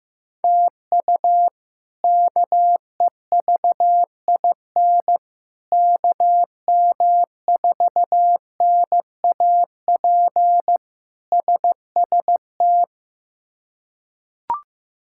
Morse code, heard as a keyed tone, about 15 wpm, 700 Hz. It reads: TU KEVIN KM4NAP SST
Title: K1USN SST QSO Practice - 15wpm